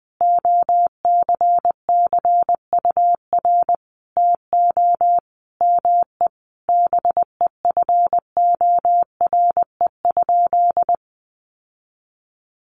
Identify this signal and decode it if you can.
Morse code, heard as a keyed tone, about 20 wpm, 700 Hz. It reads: OCCUR TO ME BEFORE?
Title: Sentences from Top 900 Words 20wpm